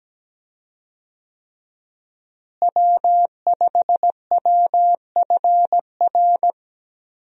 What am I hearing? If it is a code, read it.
Morse code, heard as a keyed tone, about 17 wpm, 700 Hz. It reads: W5WFR
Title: USA Call Signs - 5 Characters - Repeated 3x - 17wpm